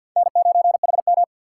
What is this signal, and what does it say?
Morse code, heard as a keyed tone, about 51 wpm, 700 Hz. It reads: N9HM
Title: USA Call Signs - 4 Characters - 3x Character Spacing - 50wpm